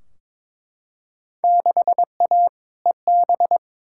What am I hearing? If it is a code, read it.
Morse code, heard as a keyed tone, about 21 wpm, 700 Hz. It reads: 6A EB
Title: Field Day Exchanges Encore - 22wpm